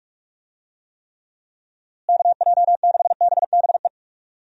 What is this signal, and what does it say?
Morse code, heard as a keyed tone, about 41 wpm, 700 Hz. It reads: KJ6BBE
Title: USA Call Signs - 6 Characters - Repeated 3x - 45wpm